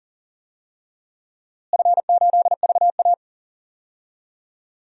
Morse code, heard as a keyed tone, about 39 wpm, 700 Hz. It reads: F8VA
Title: DXCC-100 Call Signs - 4 Characters - Repeated 2x - 40wpm